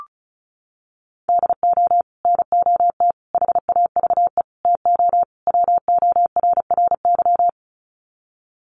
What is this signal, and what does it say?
Morse code, heard as a keyed tone, about 34 wpm, 700 Hz. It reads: DO NOT HAVE TO WORRY
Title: Sets of 5 Words 35wpm